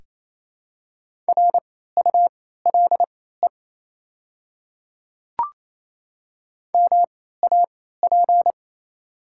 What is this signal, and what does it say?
Morse code, heard as a keyed tone, about 27 wpm, spaced as for 16 wpm, 700 Hz. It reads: RULE MAP
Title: 3x Character Spacing - Top 500 Words - 28wpm